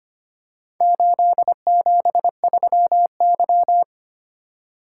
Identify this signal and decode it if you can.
Morse code, heard as a keyed tone, about 25 wpm, 700 Hz. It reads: 873Y
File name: practice / 25wpm / Four Letter-Number Combinations - 25wpm